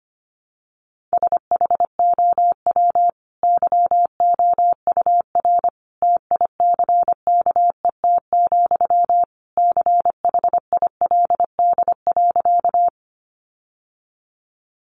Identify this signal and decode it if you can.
Morse code, heard as a keyed tone, about 25 wpm, 700 Hz. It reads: SHOW YOUR TICKET, CHILD.
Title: Sentences from Top 1400 Words 25wpm